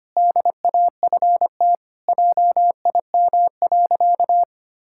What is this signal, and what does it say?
Morse code, heard as a keyed tone, about 25 wpm, 700 Hz. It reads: DAFT JIM.
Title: Pangram Challenge v2 - 25wpm